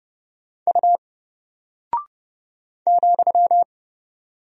Morse code, heard as a keyed tone, about 29 wpm, 700 Hz. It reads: U ,
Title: Single Character (Letter:Number:Punctuation:Prosign) 30wpm